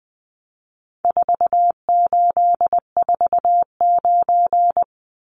Morse code, heard as a keyed tone, about 20 wpm, 700 Hz. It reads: 4849
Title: Numbers - 4-Digit 20wpm